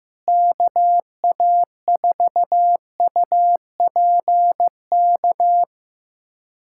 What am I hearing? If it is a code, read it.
Morse code, heard as a keyed tone, about 16 wpm, 700 Hz. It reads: KA4UPK
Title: USA Callsign DE Callsign 15wpm